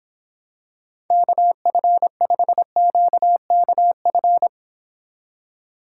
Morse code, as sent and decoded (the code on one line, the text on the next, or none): -.- ..-. ..... --.- -.- ..-.
KF5QKF